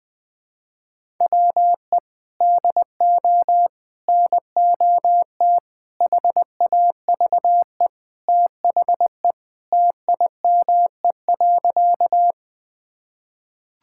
Morse code, heard as a keyed tone, about 21 wpm, 700 Hz. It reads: WE DO NOT HAVE THE TIME.
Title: Sentences with 5-Word Phrases - Spoken Beginning - 20wpm